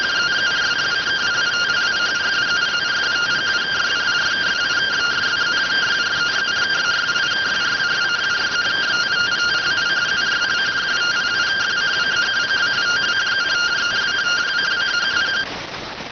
Two-tone teleprinter signal, about 170 Hz shift.